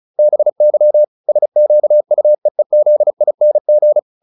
Morse code, heard as a keyed tone, about 35 wpm, 600 Hz. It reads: BY SQUEEZING